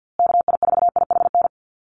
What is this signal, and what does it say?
Morse code, heard as a keyed tone, about 48 wpm, 700 Hz. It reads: KI4IHN
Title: USA Call Signs Repeated 3x 50wpm